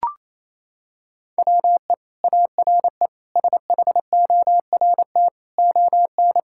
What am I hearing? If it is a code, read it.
Morse code, heard as a keyed tone, about 27 wpm, 700 Hz. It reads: WE ARE SHORT ON
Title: Sentences from Top 1000 Words - Review Part 5 - 28wpm